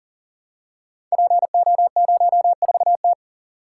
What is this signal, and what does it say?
Morse code, heard as a keyed tone, about 40 wpm, 700 Hz. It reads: PO04T
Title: Code Groups - 5 Letter:Numbers - 40wpm